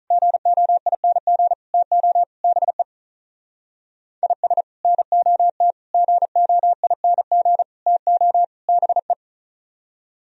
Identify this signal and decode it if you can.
Morse code, heard as a keyed tone, about 44 wpm, 700 Hz. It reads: GOING TO BE IS NOT GOING TO BE